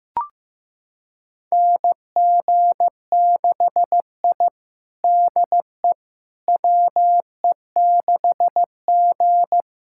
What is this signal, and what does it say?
Morse code, heard as a keyed tone, about 15 wpm, 700 Hz. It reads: NG6I DE WE6G
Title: USA Callsign DE Callsign - 4 Characters - 15wpm